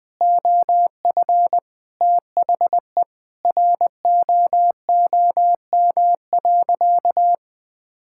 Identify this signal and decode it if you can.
Morse code, heard as a keyed tone, about 21 wpm, 700 Hz. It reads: OF THE ROOM.
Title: Sentences with 5-Word Phrases - Spoken Beginning - 20wpm